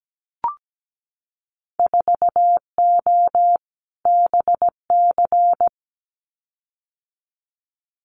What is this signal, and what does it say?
Morse code, heard as a keyed tone, about 17 wpm, 700 Hz. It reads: 4O BC
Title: Winter Field Day Exchanges Encore 17wpm